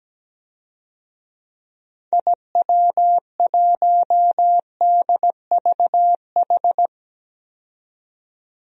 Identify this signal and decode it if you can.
Morse code, heard as a keyed tone, about 17 wpm, 700 Hz. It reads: IW1DVH
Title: DXCC-100 Call Signs - 6 Characters - 17wpm